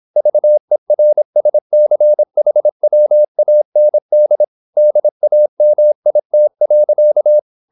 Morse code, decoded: VERSCHWAND DAMIT.

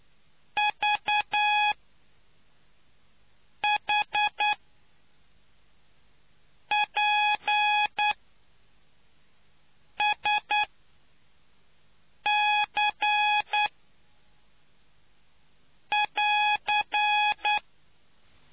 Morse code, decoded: VHPSC<AR>

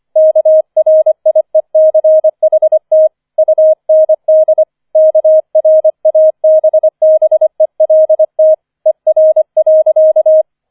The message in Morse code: KRIECHT UND KRABBELT ER.

